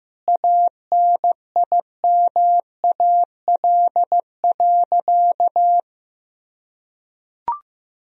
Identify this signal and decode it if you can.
Morse code, heard as a keyed tone, about 16 wpm, 700 Hz. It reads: ANIMAL.
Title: Sentences from Top 1000 Words - Review Part 1 - 15wpm